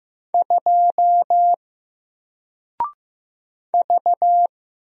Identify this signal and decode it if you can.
Morse code, heard as a keyed tone, about 15 wpm, 700 Hz. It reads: 2 V
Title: Single Character (Letter:Number:Punctuation:Prosign) Alt 15wpm